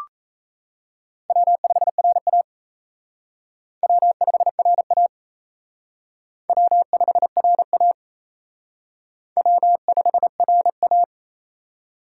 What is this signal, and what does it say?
Morse code, heard as a keyed tone, about 36 wpm, 700 Hz. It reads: W5RA W5RA W5RA W5RA